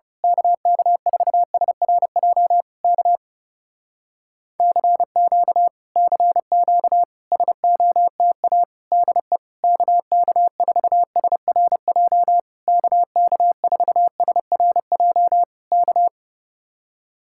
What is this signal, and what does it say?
Morse code, heard as a keyed tone, about 29 wpm, 700 Hz. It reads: KK4SRJ K CQ CQ SOTA DE KK4SRJ KK4SRJ K
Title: Speed-Racing Common QSO Phrases - 30wpm